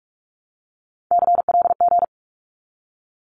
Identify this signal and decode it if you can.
Morse code, heard as a keyed tone, about 46 wpm, 700 Hz. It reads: CLG
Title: Abbreviations 45wpm